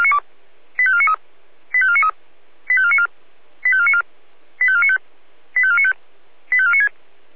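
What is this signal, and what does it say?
Five-tone selective call (ZVEI), heard as quick runs of five tones, tones of about 70 ms. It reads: 85493 85493 85494 85494 85495 85495 85496
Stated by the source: Five-tone ZVEI selective call 85491, received by SDR.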